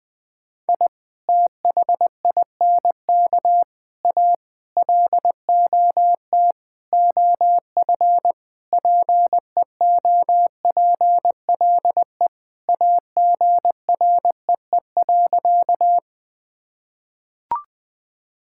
Morse code, heard as a keyed tone, about 20 wpm, 700 Hz. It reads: I THINK A LOT OF PEOPLE AGREE.
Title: Sentences with 5-Word Phrases - Spoken Beginning - 20wpm